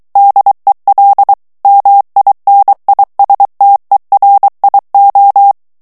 Morse code, heard as a keyed tone, about 23 wpm, 800 Hz. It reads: DEL MINISTERIO
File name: morse mp3 / ESPAGNOL TEXTE EN CLAIR 2 VITESSE 22 MOTS MINUTES